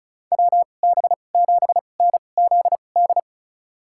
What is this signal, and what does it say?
Morse code, heard as a keyed tone, about 34 wpm, spaced as for 25 wpm, 700 Hz. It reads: WB7NZD
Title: USA Call Signs - 6 Characters - 2x Character Spacing - 35wpm